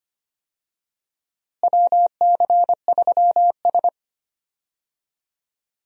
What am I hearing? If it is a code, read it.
Morse code, heard as a keyed tone, about 25 wpm, 700 Hz. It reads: WC3S
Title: USA Call Signs - 4 Characters - Repeated 3x - 25wpm